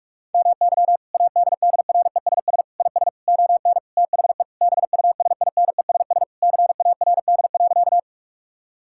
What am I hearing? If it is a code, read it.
Morse code, heard as a keyed tone, about 41 wpm, 700 Hz. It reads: MY ADDRESS IS ON THE BUSINESS CARD.
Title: Sentences from Top 1000 Words - Review Part 6 - 45wpm